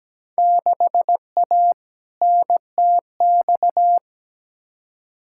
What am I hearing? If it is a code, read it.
Morse code, heard as a keyed tone, about 17 wpm, 700 Hz. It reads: 6A NTX
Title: Field Day Exchanges - 17wpm